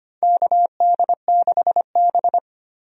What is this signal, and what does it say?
Morse code, heard as a keyed tone, about 25 wpm, 700 Hz. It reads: KD6B
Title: USA Call Signs - 4 Characters - 25wpm